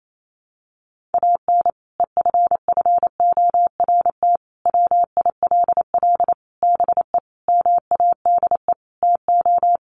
Morse code, heard as a keyed tone, about 28 wpm, 700 Hz. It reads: AN EFFORT WILL BE MADE TO
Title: Sentences from Top 1500 Words 28wpm